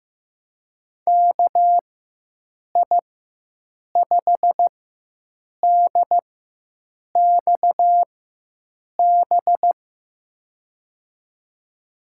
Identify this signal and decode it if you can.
Morse code, heard as a keyed tone, about 15 wpm, spaced as for 7 wpm, 700 Hz. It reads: KI5DXB
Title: USA Call Signs - 6 Characters - 4x Character Spacing - 15wpm